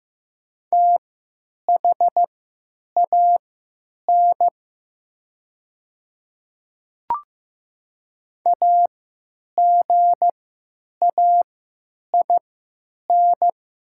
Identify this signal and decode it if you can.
Morse code, heard as a keyed tone, about 15 wpm, spaced as for 9 wpm, 700 Hz. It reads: THAN AGAIN
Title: 3x Character Spacing - Top 200 Words - 15wpm